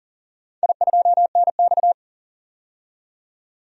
Morse code, heard as a keyed tone, about 39 wpm, 700 Hz. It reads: I2NX
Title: DXCC-100 Call Signs - 4 Characters - 40wpm